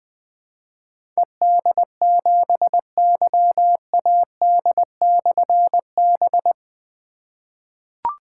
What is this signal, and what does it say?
Morse code, heard as a keyed tone, about 20 wpm, 700 Hz. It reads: ED7YAD/B
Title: Difficult Call Signs 20wpm